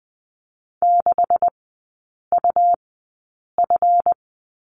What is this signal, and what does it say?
Morse code, heard as a keyed tone, about 20 wpm, spaced as for 8 wpm, 700 Hz. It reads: 6UF
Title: 138 Lesson 23e - Letter f - Three Characters 20wpm